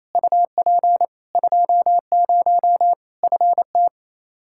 Morse code, heard as a keyed tone, about 27 wpm, 700 Hz. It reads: UP 20 FT
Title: Practice QSO - Exchange 2 - Info - 28wpm